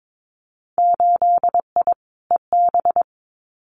Morse code, heard as a keyed tone, about 22 wpm, 700 Hz. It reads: 8I EB
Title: Winter Field Day Exchanges Encore 22wpm